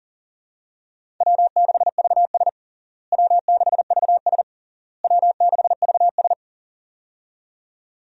Morse code, heard as a keyed tone, about 40 wpm, 700 Hz. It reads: W6VS W6VS W6VS
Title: USA Call Signs - 4 Characters - Repeated 3x - 40wpm